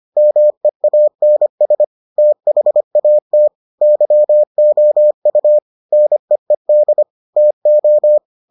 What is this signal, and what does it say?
Morse code, heard as a keyed tone, about 25 wpm, 600 Hz. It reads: MEANS THAT YOU NEED TO